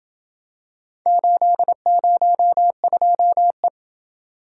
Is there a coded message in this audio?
Yes: Morse code from 1.1 to 3.7 s, heard as a keyed tone, about 27 wpm, 700 Hz. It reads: 802E